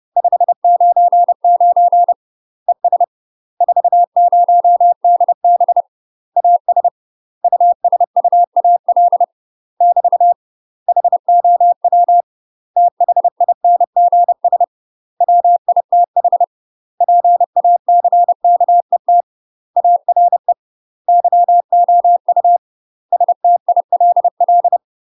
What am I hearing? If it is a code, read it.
Morse code, heard as a keyed tone, about 30 wpm, 700 Hz. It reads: 599 ES 40DB AS USUAL <BT> HOW THINGS WITH PACKET ARE YOU STILL
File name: UK MASTER QSOs TRAINING / 30WPM / UK MASTER QSOs 0021